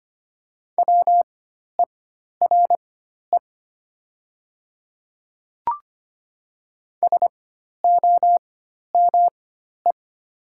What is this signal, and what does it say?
Morse code, heard as a keyed tone, about 25 wpm, spaced as for 12 wpm, 700 Hz. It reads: WERE SOME
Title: 4x Character Spacing - Top 100 Words - 25wpm